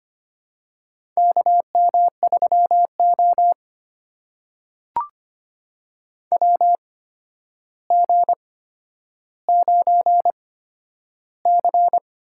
Morse code, heard as a keyed tone, about 25 wpm, spaced as for 11 wpm, 700 Hz. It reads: KM3O WG9C
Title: USA Call Signs - 4 Characters - 8x Character Spacing - 25wpm